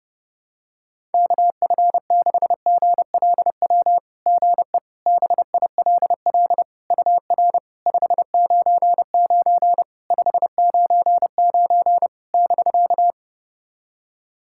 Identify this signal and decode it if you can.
Morse code, heard as a keyed tone, about 30 wpm, 700 Hz. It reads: KF6GLW GE BILL UR 599 599 <BK>
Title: SOTA QSO Practice 30wpm